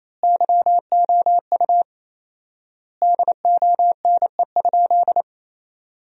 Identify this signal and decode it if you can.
Morse code, heard as a keyed tone, about 28 wpm, 700 Hz. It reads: YOU DONE?
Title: 4x Word-Spacing - Sentences from Top 500 Words - 28wpm